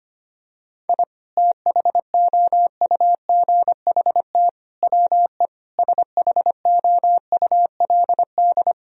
Morse code, heard as a keyed tone, about 25 wpm, 700 Hz. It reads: I THOUGHT WE SHOULD
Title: Sentences from Top 1000 Words - Review Part 2 - 25wpm